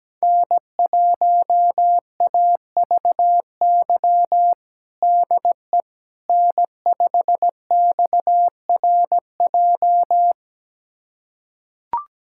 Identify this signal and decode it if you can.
Morse code, heard as a keyed tone, about 17 wpm, 700 Hz. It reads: N1AVY DE N5XRJ
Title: USA Callsign DE Callsign - 5 Characters - 17wpm